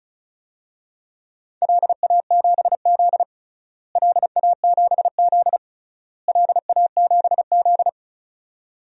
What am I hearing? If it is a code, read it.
Morse code, heard as a keyed tone, about 34 wpm, 700 Hz. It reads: LA7Z LA7Z LA7Z
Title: DXCC-100 Call Signs - 4 Characters - Repeated 3x - 35wpm